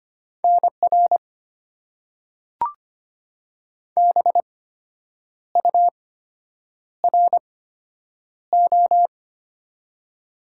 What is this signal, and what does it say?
Morse code, heard as a keyed tone, about 25 wpm, spaced as for 8 wpm, 700 Hz. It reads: NR BURO
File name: Top 90 QSO Elements - 8x Character Spacing - 25wpm